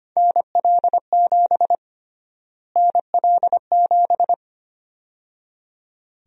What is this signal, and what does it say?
Morse code, heard as a keyed tone, about 25 wpm, 700 Hz. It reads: NL7 NL7
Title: USA Call Sign Prefixes - Repeated 2x - 25wpm